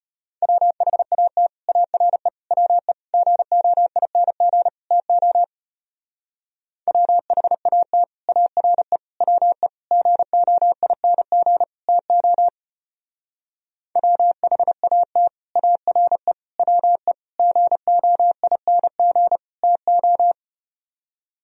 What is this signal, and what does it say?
Morse code, heard as a keyed tone, about 33 wpm, 700 Hz. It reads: WHAT ARE WE GOING TO WHAT ARE WE GOING TO WHAT ARE WE GOING TO